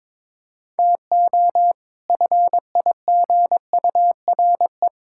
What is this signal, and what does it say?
Morse code, heard as a keyed tone, about 22 wpm, 700 Hz. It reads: TO FIGURE
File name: practice / 22wpm / Build A Phrase - 5-Word Phrases - 22wpm